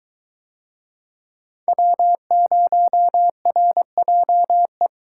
Morse code, heard as a keyed tone, about 21 wpm, 700 Hz. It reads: W0RJE